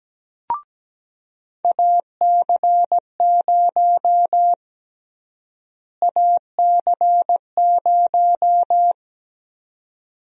Morse code, heard as a keyed tone, about 17 wpm, 700 Hz. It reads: AC0 AC0
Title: USA Call Sign Prefixes - Repeated 2x - 17wpm